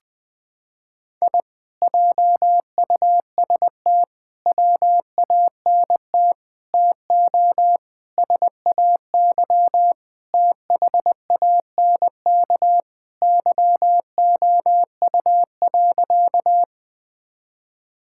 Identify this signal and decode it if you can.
Morse code, heard as a keyed tone, about 20 wpm, 700 Hz. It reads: I JUST WANT TO SAY THANK YOU.
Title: Sentences with 2-Word Phrases - 20wpm